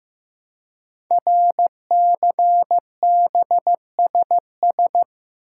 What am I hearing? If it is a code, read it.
Morse code, heard as a keyed tone, about 15 wpm, 700 Hz. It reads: RCBSS
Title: Code Groups - 5 Letters - 15wpm